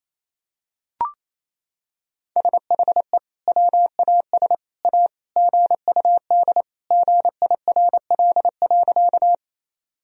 Morse code, heard as a keyed tone, about 28 wpm, 700 Hz. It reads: SHE WAS A GUD GIRL.
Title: Sentences from Top 400 Words and Abbreviations 28wpm